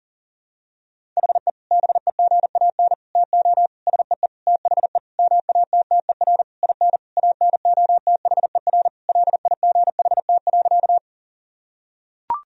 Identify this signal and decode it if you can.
Morse code, heard as a keyed tone, about 40 wpm, 700 Hz. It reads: HE BEGAN TO SEE THE MATTER IN ANOTHER LIGHT.
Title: Sentences from Top 1000 Words - Review Part 1 - 40wpm